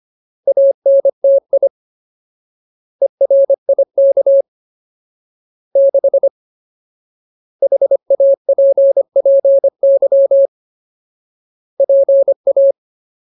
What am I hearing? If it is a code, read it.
Morse code, heard as a keyed tone, about 25 wpm, 550 Hz. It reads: ANTI ERIK 6 HAPPY PA
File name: Ultimate Word List / 25WPM Ultimate